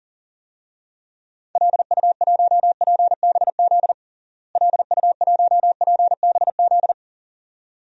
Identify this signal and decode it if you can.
Morse code, heard as a keyed tone, about 40 wpm, 700 Hz. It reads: LU1PBZ LU1PBZ
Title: DXCC-100 Call Signs - 6 Characters - Repeated 2x - 40wpm